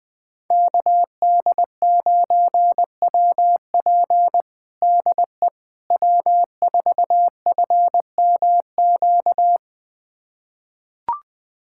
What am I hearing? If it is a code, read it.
Morse code, heard as a keyed tone, about 20 wpm, 700 Hz. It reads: KD9WP DE W4FMQ
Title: USA Callsign DE Callsign - 5 Characters - 20wpm